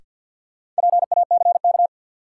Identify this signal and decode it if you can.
Morse code, heard as a keyed tone, about 53 wpm, 700 Hz. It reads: PACK
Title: Sentences from Top 1300 Words - New Word Drill Alt - 50wpm